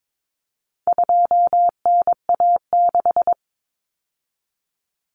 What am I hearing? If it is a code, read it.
Morse code, heard as a keyed tone, about 22 wpm, 700 Hz. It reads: 2NA6
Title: Common Cut Numbers - 4-Digit 22wpm